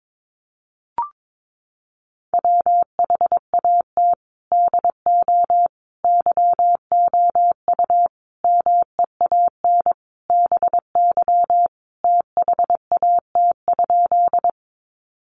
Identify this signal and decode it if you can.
Morse code, heard as a keyed tone, about 22 wpm, 700 Hz. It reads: WHAT DO YOU MEAN BY THAT?
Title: Sentences from Top 200 Words 22wpm